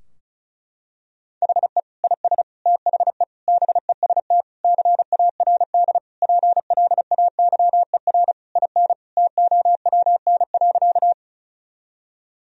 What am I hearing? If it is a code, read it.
Morse code, heard as a keyed tone, about 34 wpm, 700 Hz. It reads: HE IS THE BEST CARD PLAYER IN TOWN.
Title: Sentences from Top 1000 Words - Review Part 4 - 35wpm